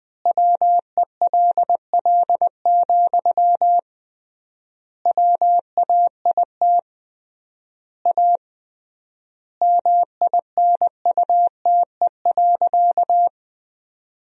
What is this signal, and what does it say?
Morse code, heard as a keyed tone, about 20 wpm, 700 Hz. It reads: WELL, WAIT A MINUTE.
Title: 3x Word-Spacing - Sentences from Top 500 Words - 20wpm